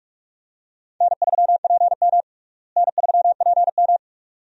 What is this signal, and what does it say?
Morse code, heard as a keyed tone, about 46 wpm, 700 Hz. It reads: N3PM N3PM
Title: USA Call Signs - 4 Characters - Repeated 2x - 45wpm